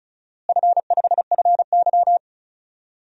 Morse code, heard as a keyed tone, about 34 wpm, 700 Hz. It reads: F5FY
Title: DXCC-100 Call Signs - 4 Characters - Repeated 2x - 35wpm